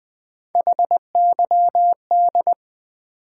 Morse code, heard as a keyed tone, about 20 wpm, 700 Hz. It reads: HYD
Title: Build A Word - 3 Letters At A Time - 8-Letter Words - 20wpm